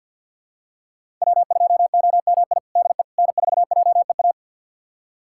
Morse code, heard as a keyed tone, about 52 wpm, 700 Hz. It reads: W2OMI DE N4JEA
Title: USA Callsign DE Callsign - 5 Characters - 50wpm